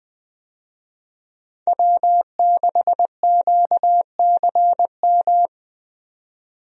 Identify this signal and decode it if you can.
Morse code, heard as a keyed tone, about 20 wpm, 700 Hz. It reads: W6QCM
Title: USA Call Signs - 5 Characters - 20wpm